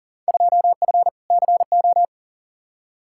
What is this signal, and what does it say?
Morse code, heard as a keyed tone, about 37 wpm, 700 Hz. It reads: 2F CO
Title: Field Day Exchanges - 40wpm